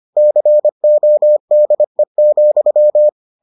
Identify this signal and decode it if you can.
Morse code, heard as a keyed tone, about 25 wpm, 600 Hz. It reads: CODE,